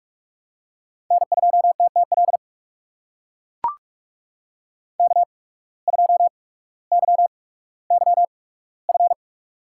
Morse code, heard as a keyed tone, about 44 wpm, spaced as for 19 wpm, 700 Hz. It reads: N2TTL K2YYF
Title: USA Call Signs - 5 Characters - 8x Character Spacing - 45wpm